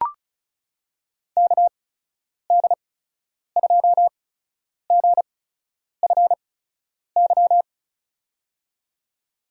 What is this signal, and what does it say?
Morse code, heard as a keyed tone, about 36 wpm, spaced as for 10 wpm, 700 Hz. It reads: KD2GFY
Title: USA Call Signs - 6 Characters - 8x Character Spacing - 35wpm